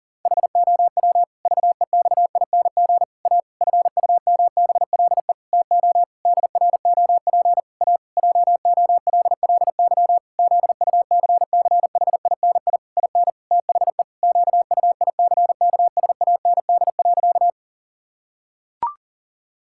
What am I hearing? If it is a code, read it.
Morse code, heard as a keyed tone, about 40 wpm, 700 Hz. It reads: HOW VEXING A FUMBLE TO DROP A JOLLY ZUCCHINI IN THE QUICKSAND.
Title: Pangram Challenge v2 - 40wpm